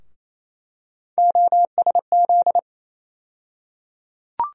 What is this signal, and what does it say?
Morse code, heard as a keyed tone, about 28 wpm, 700 Hz. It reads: OSZ
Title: USA Call Sign Postfixes - 3 Letters 28wpm